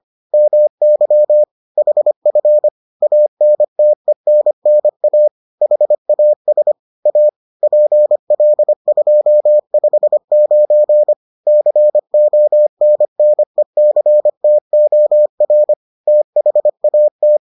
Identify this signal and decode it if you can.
Morse code, heard as a keyed tone, about 25 wpm, 600 Hz. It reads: MY HF ANTENNA HAS A PL259 CONNECTOR THAT